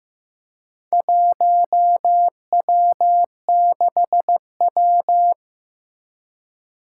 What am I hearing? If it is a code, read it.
Morse code, heard as a keyed tone, about 15 wpm, 700 Hz. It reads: 1W6W
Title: Four Challenging Letter-Number Combinations 15wpm